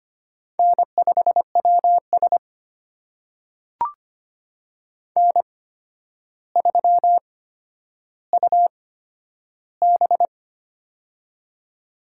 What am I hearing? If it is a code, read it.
Morse code, heard as a keyed tone, about 25 wpm, spaced as for 11 wpm, 700 Hz. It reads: N5WS N3UB
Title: USA Call Signs - 4 Characters - 8x Character Spacing - 25wpm